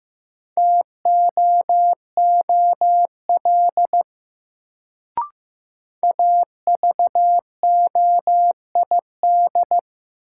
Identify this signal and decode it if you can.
Morse code, heard as a keyed tone, about 15 wpm, 700 Hz. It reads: TOOL AVOID
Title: Sentences from Top 1000 Words - Review Part 6 - New Word Drill - 15wpm